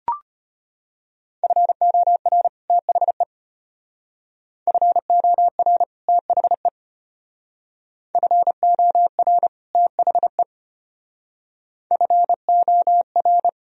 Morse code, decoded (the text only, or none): FOR THE FOR THE FOR THE FOR